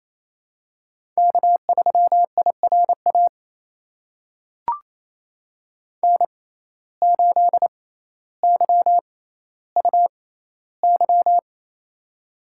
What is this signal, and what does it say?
Morse code, heard as a keyed tone, about 28 wpm, spaced as for 14 wpm, 700 Hz. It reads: K3IRA N8YUY
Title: USA Call Signs - 5 Characters - 6x Character Spacing - 28wpm